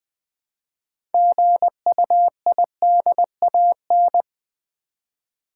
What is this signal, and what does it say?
Morse code, heard as a keyed tone, about 20 wpm, 700 Hz. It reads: GUIDAN
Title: Build A Word - 2 Letters At A Time - 8-Letter Words - 20wpm